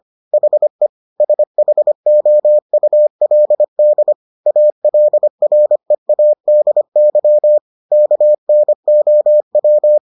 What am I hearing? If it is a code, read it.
Morse code, heard as a keyed tone, about 25 wpm, 600 Hz. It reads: HE SHOULD ALREADY KNOW